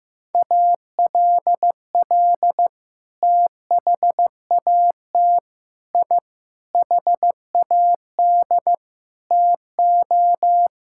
Morse code, heard as a keyed tone, about 16 wpm, 700 Hz. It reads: ALL THAT I HAD TO
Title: Sentences from Top 1000 Words - Review Part 6 - 15wpm